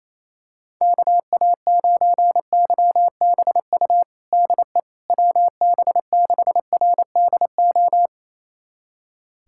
Morse code, heard as a keyed tone, about 28 wpm, 700 Hz. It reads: KA9YBU DE WB6RDO
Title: USA Callsign DE Callsign 28wpm